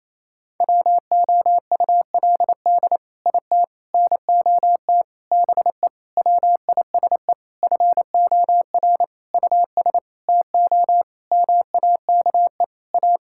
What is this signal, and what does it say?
Morse code, heard as a keyed tone, about 28 wpm, 700 Hz. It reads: WOULD IT NOT BE WISE FOR US TO MAKE A
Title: Sentences from Top 1600 Words 28wpm